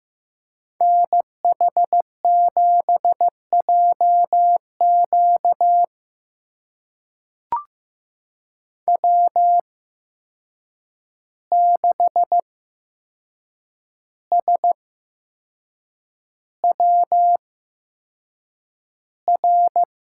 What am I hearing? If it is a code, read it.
Morse code, heard as a keyed tone, about 15 wpm, spaced as for 6 wpm, 700 Hz. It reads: NH7JQ W6SWR
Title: USA Call Signs - 5 Characters - 8x Character Spacing - 15wpm